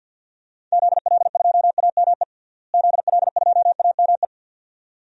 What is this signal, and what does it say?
Morse code, heard as a keyed tone, about 47 wpm, 700 Hz. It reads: ZL2AME ZL2AME
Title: DXCC-100 Call Signs - 6 Characters - Repeated 2x - 50wpm